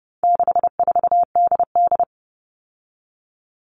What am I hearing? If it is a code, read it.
Morse code, heard as a keyed tone, about 30 wpm, 700 Hz. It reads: 64DD
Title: All Cut Numbers - 4-Digit 30wpm